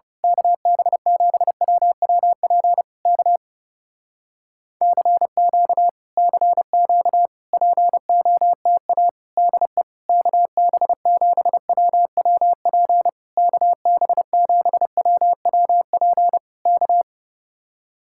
Morse code, heard as a keyed tone, about 28 wpm, 700 Hz. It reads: KB7WWP K CQ CQ POTA DE KB7WWP KB7WWP K
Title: Speed-Racing Common QSO Phrases Encore - 30wpm